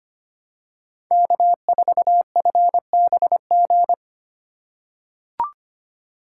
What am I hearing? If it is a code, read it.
Morse code, heard as a keyed tone, about 25 wpm, 700 Hz. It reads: K4FBG
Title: USA Call Signs - 5 Characters - 3x Character Spacing - 25wpm